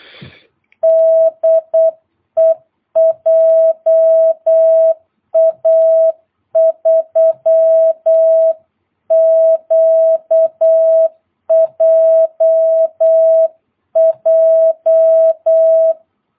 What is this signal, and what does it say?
Morse code, heard as a keyed tone, about 8 wpm, 660 Hz. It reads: DE JA3QJJ
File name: jh8cbh / 2022cw